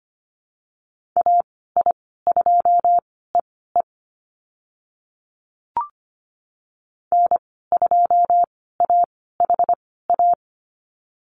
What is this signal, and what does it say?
Morse code, heard as a keyed tone, about 25 wpm, spaced as for 16 wpm, 700 Hz. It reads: AI2EE N2AHA
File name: Lesson 15g - Number 5 - Call Signs 25wpm